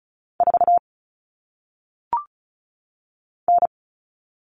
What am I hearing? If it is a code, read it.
Morse code, heard as a keyed tone, about 36 wpm, 700 Hz. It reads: 4 N
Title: All Cut Numbers - 1-Digit 35wpm